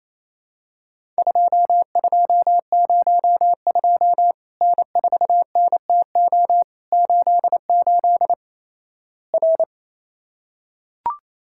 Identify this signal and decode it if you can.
Morse code, heard as a keyed tone, about 28 wpm, 700 Hz. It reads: 2202 N4NTO 88 R
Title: Worked All Europe DX Contest - QTC Exchange - 28wpm